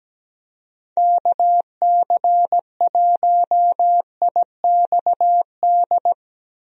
Morse code, heard as a keyed tone, about 17 wpm, 700 Hz. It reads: KC1IXD
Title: USA Call Signs - 6 Characters - Repeated 3x - 17wpm